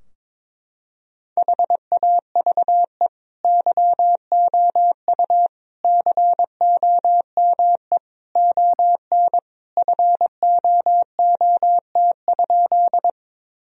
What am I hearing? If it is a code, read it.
Morse code, heard as a keyed tone, about 22 wpm, 700 Hz. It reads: HAVE YOU COME ON FOOT?
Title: Sentences from Top 1000 Words - Review Part 3 - 22wpm